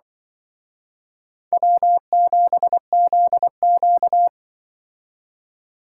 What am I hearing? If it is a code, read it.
Morse code, heard as a keyed tone, about 24 wpm, 700 Hz. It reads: W7ZQ
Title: Speed Racing - USA Call Signs 20wpm